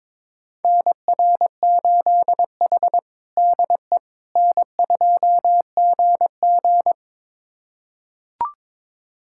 Morse code, heard as a keyed tone, about 22 wpm, 700 Hz. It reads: NR8H DE N2GG
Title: USA Callsign DE Callsign - 4 Characters - 22wpm